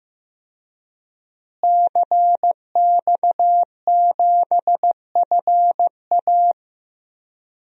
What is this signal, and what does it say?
Morse code, heard as a keyed tone, about 15 wpm, 700 Hz. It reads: CX7FA
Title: DXCC-100 Call Signs - 5 Characters - Repeated 3x - 15wpm